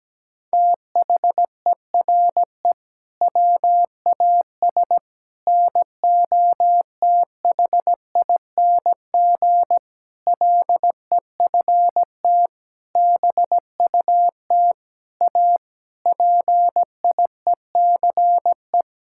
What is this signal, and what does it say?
Morse code, heard as a keyed tone, about 17 wpm, 700 Hz. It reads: THERE WAS NOTHING LEFT BUT A PIECE
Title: Sentences from Top 1600 Words 17wpm